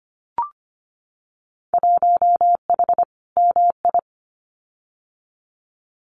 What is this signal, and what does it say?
Morse code, heard as a keyed tone, about 25 wpm, 700 Hz. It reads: 1H MI
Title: Winter Field Day Exchanges 25wpm